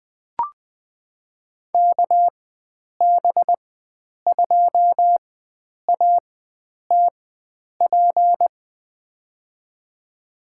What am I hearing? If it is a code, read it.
Morse code, heard as a keyed tone, about 20 wpm, spaced as for 9 wpm, 700 Hz. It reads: KB2ATP
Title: USA Call Signs - 4x Character Spacing - 20wpm